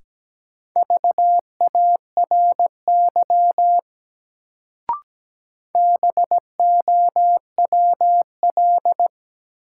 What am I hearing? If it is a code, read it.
Morse code, heard as a keyed tone, about 17 wpm, 700 Hz. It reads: VARY BOWL
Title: Sentences from Top 1200 Words - New Word Drill Alt - 17wpm